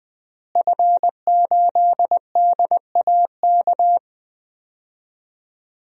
Morse code, heard as a keyed tone, about 20 wpm, 700 Hz. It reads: F8DAK
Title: DXCC-100 Call Signs - 5 Characters - Repeated 3x - 20wpm